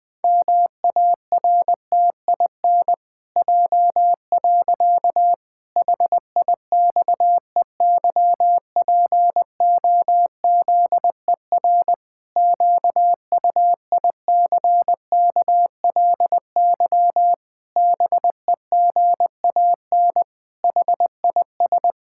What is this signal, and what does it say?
Morse code, heard as a keyed tone, about 21 wpm, 700 Hz. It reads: MARTIN J. HIXEYPOZER QUICKLY BEGAN HIS
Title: Pangram Challenge v2 - 20wpm